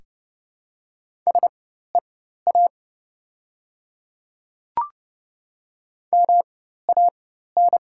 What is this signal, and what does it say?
Morse code, heard as a keyed tone, about 30 wpm, spaced as for 14 wpm, 700 Hz. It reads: SEA MAN
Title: 4x Character Spacing - Top 500 Words - 30wpm